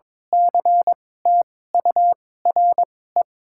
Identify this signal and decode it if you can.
Morse code, heard as a keyed tone, about 22 wpm, spaced as for 16 wpm, 700 Hz. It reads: CTURE
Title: 2x Character Spacing - Top 200 Words - 22wpm